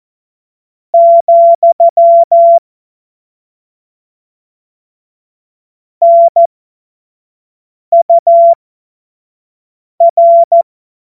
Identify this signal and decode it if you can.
Morse code, heard as a keyed tone, about 14 wpm, spaced as for 5 wpm, 680 Hz. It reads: , NUR
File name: Morsekurs / Witze / Witz 07. Februar 2023 25 F70